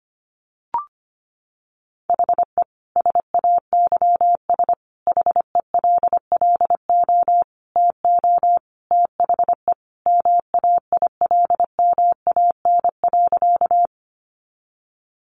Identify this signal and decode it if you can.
Morse code, heard as a keyed tone, about 25 wpm, 700 Hz. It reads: HE SAYS HELLO TO THE MAILMAN.
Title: Additional Easy Phrases 25wpm